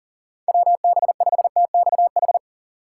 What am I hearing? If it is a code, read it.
Morse code, heard as a keyed tone, about 40 wpm, 700 Hz. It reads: WB5TXH
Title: USA Call Signs - 6 Characters - 4x Character Spacing - 40wpm